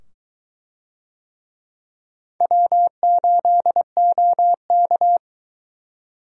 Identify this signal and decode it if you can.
Morse code, heard as a keyed tone, about 22 wpm, 700 Hz. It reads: W8OK